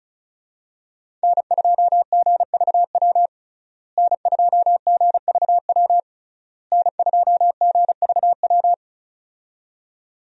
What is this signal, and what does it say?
Morse code, heard as a keyed tone, about 36 wpm, 700 Hz. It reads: N2GVW N2GVW N2GVW
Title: USA Call Signs - 5 Characters - Repeated 3x - 35wpm